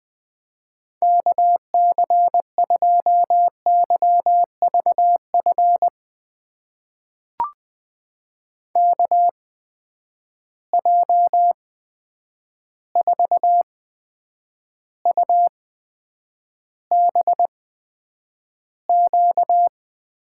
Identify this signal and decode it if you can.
Morse code, heard as a keyed tone, about 20 wpm, spaced as for 9 wpm, 700 Hz. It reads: KC2YVF KJ4UBQ
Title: USA Call Signs - 6 Characters - 8x Character Spacing - 20wpm